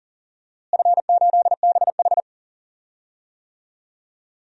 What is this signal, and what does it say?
Morse code, heard as a keyed tone, about 40 wpm, 700 Hz. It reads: F8BH
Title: DXCC-100 Call Signs - 4 Characters - Repeated 3x - 40wpm